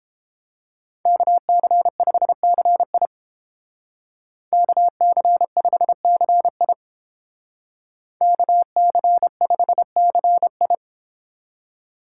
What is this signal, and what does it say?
Morse code, heard as a keyed tone, about 29 wpm, 700 Hz. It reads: KC5CI KC5CI KC5CI